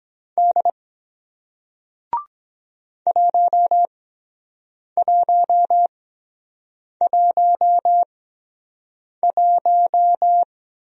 Morse code, heard as a keyed tone, about 23 wpm, spaced as for 7 wpm, 700 Hz. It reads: D 1111